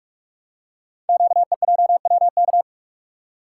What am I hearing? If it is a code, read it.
Morse code, heard as a keyed tone, about 44 wpm, 700 Hz. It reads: QEJWK
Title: Code Groups - 5 Letters - 45wpm